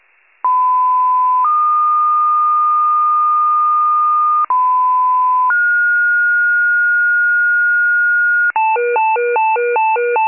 Two-tone page, heard as a one-second tone, then a three-second tone, about 1010 then 1220 Hz, about 1010 then 1430 Hz, with noise. It is heard from 0.4 to 8.5 s.